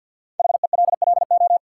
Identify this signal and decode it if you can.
Morse code, heard as a keyed tone, about 53 wpm, 700 Hz. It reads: HELLO
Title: Sentences from Top 1100 Words - New Word Drill Alt - 50wpm